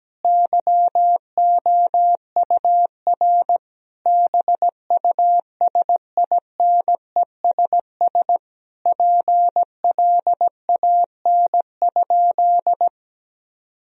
Morse code, heard as a keyed tone, about 17 wpm, 700 Hz. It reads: YOUR BUSINESS PLAN?
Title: Sentences from Top 1000 Words - Review Part 3 - 17wpm